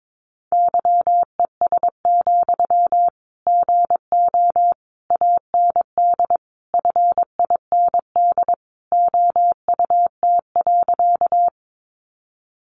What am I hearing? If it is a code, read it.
Morse code, heard as a keyed tone, about 22 wpm, 700 Hz. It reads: YES, GO AND FIND OUT.
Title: Sentences from Top 500 Words 22wpm